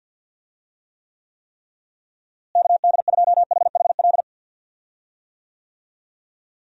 Morse code, heard as a keyed tone, about 46 wpm, 700 Hz. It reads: KD2HHL
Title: USA Call Signs - 6 Characters - Repeated 3x - 50wpm